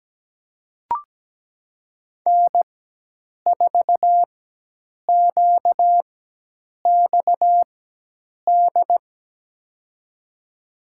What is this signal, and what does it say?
Morse code, heard as a keyed tone, about 17 wpm, spaced as for 8 wpm, 700 Hz. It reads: N4QXD
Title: USA Call Signs - 5 Characters - 4x Character Spacing - 17wpm